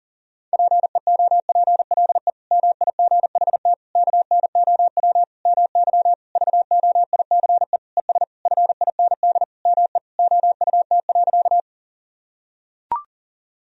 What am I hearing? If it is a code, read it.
Morse code, heard as a keyed tone, about 40 wpm, 700 Hz. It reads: PEOPLE MIGHT KNOW MY VOICE ES FIND ME OUT.
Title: Sentences from Top 500 Words and Abbreviations 40wpm